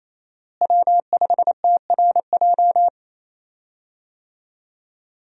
Morse code, heard as a keyed tone, about 28 wpm, 700 Hz. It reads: W5TRJ
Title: USA Call Signs - 5 Characters - Repeated 3x - 28wpm